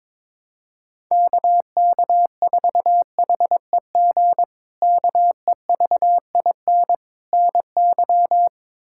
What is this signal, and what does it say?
Morse code, heard as a keyed tone, about 22 wpm, 700 Hz. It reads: KK4HEG KEVIN NY
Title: K1USN SST QSO Practice - 22wpm